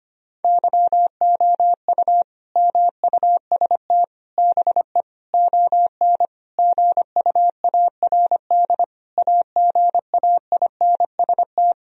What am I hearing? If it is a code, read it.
Morse code, heard as a keyed tone, about 25 wpm, 700 Hz. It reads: YOU MUST BE ON GUARD AGAINST
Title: Sentences from Top 1000 Words - Review Part 4 - 25wpm